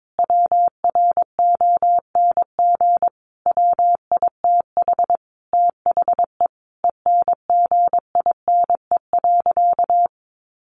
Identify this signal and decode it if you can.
Morse code, heard as a keyed tone, about 22 wpm, 700 Hz. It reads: WRONG WITH THE ENGINE.
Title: Sentences from Top 1500 Words 22wpm